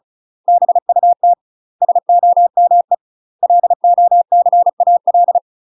Morse code, heard as a keyed tone, about 35 wpm, 700 Hz. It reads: BUT SOME LOCAL
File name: UK MASTER QSOs TRAINING / 35WPM / UK MASTER QSOs 0040